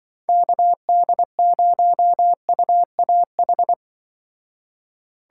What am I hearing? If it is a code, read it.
Morse code, heard as a keyed tone, about 24 wpm, 700 Hz. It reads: KD0UAH
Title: Speed Racing - USA Call Signs 20wpm